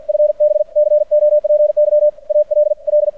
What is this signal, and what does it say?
Morse code, heard as a keyed tone, about 53 wpm, 600 Hz. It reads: FB COPY ALL
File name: archived / CRSnet